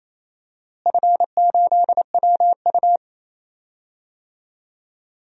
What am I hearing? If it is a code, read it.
Morse code, heard as a keyed tone, about 28 wpm, 700 Hz. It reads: F8WU
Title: DXCC-100 Call Signs - 4 Characters - Repeated 3x - 28wpm